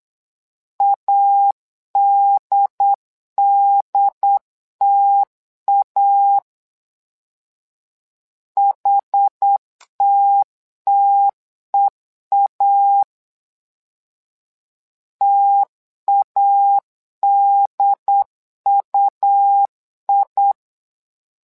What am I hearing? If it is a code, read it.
Morse code, heard as a keyed tone, about 8 wpm, 800 Hz. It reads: ADDTA HTTEA TADUI
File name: LEZIONI / 2 UHDN / 2 UHD 8wpm 2258